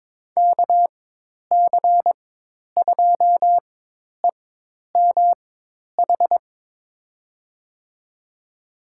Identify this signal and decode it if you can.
Morse code, heard as a keyed tone, about 21 wpm, spaced as for 10 wpm, 700 Hz. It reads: KC2EMH
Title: USA Call Signs - 6 Characters - 4x Character Spacing - 22wpm